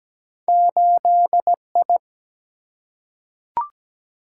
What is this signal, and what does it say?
Morse code, heard as a keyed tone, about 17 wpm, 700 Hz. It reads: 8I
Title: Single-Letter Single-Number 17wpm